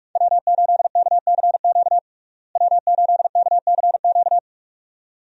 Morse code, heard as a keyed tone, about 42 wpm, 700 Hz. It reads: W8KCQ W8KCQ
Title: USA Call Signs - 5 Characters - Repeated 2x - 45wpm